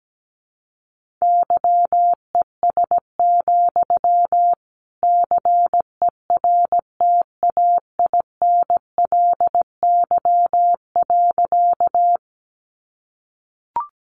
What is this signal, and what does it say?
Morse code, heard as a keyed tone, about 17 wpm, 700 Hz. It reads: YES, CERTAINLY.
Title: Sentences from Top 900 Words 17wpm